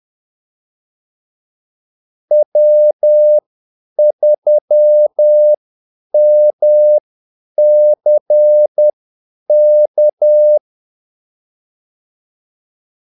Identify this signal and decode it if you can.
Morse code, heard as a keyed tone, about 10 wpm, spaced as for 8 wpm, 600 Hz. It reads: W3MCK
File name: Callsigns-10-8wpm